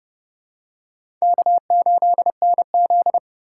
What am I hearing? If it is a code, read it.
Morse code, heard as a keyed tone, about 29 wpm, 700 Hz. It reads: K8NZ
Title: K1USN SST QSO Practice - 30wpm